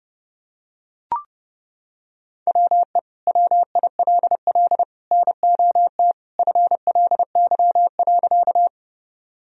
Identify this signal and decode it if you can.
Morse code, heard as a keyed tone, about 30 wpm, 700 Hz. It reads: WE WILL NOT FLY.
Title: Sentences from Top 1000 Words - Review Part 2 - 30wpm